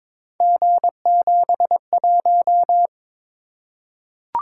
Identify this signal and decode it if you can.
Morse code, heard as a keyed tone, about 22 wpm, 700 Hz. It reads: G71
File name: practice / 22wpm / Three Letter-Number Combinations - 22wpm